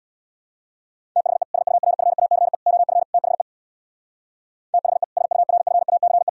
Morse code, heard as a keyed tone, about 62 wpm, 700 Hz. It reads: THE SURFACE OF THE THE SURFACE
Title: Speed-Racing - Sets of 4 Words 50wpm